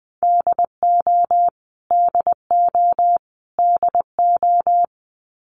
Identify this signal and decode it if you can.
Morse code, heard as a keyed tone, about 20 wpm, 700 Hz. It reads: DO DO DO
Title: Top 300 Words Repeated 20wpm